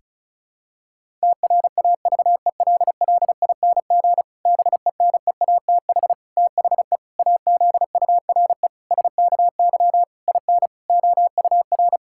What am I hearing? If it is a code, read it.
Morse code, heard as a keyed tone, about 36 wpm, 700 Hz. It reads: TRAVELLING BENEATH THE AZURE SKY IN OUR
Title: Pangram Challenge v2 - 35wpm